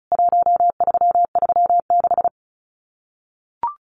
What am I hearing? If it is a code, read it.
Morse code, heard as a keyed tone, about 34 wpm, 700 Hz. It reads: 1336
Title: CWT Exchange 35wpm